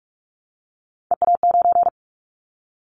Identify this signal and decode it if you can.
Morse code, heard as a keyed tone, about 46 wpm, 700 Hz. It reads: EA9
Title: Callsign Prefixes for Top 100 DXCC - 45wpm